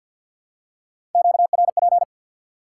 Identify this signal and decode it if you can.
Morse code, heard as a keyed tone, about 53 wpm, 700 Hz. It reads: QRP
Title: Top 90 QSO Elements 50wpm